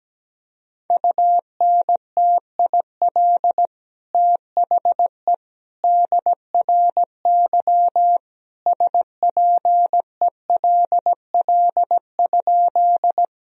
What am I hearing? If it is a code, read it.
Morse code, heard as a keyed tone, about 17 wpm, 700 Hz. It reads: UNTIL THE DRY SPELL?
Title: Sentences from Top 1000 Words - Review Part 3 - 17wpm